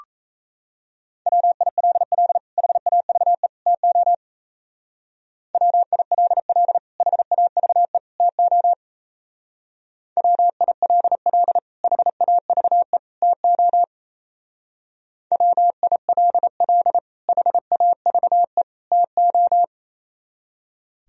Morse code, heard as a keyed tone, about 35 wpm, 700 Hz. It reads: WILL HAVE TO WILL HAVE TO WILL HAVE TO WILL HAVE TO
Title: Speed-Racing - Sets of 3 Words 28wpm